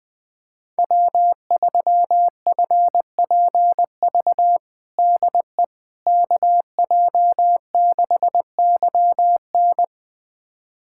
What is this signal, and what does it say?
Morse code, heard as a keyed tone, about 21 wpm, 700 Hz. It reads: W3FPV DE KJ6YN
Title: USA Callsign DE Callsign - 5 Characters - 20wpm